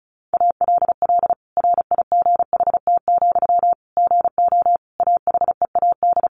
Morse code, heard as a keyed tone, about 34 wpm, 700 Hz. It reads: ALL RIGHT, GO AHEAD
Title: Sentences from Top 1100 Words 35wpm